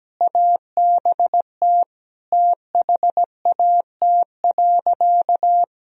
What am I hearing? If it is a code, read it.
Morse code, heard as a keyed tone, about 17 wpm, 700 Hz. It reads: ABT THAT.
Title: Sentences from Top 200 Words and Abbreviations 17wpm